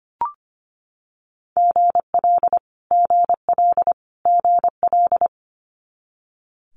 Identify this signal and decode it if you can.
Morse code, heard as a keyed tone, about 25 wpm, 700 Hz. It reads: GL GL GL
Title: Multi-letter Phonemes Repeated 3x 25wpm